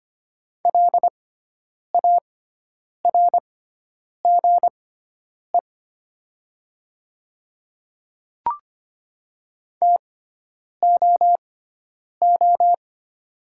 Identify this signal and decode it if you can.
Morse code, heard as a keyed tone, about 25 wpm, spaced as for 9 wpm, 690 Hz. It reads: LARGE TOO
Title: 6x Character Spacing - Top 200 Words - 25wpm